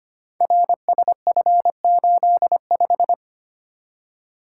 Morse code, heard as a keyed tone, about 25 wpm, 700 Hz. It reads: RSF85
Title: Code Groups - 5 Letter:Numbers - 25wpm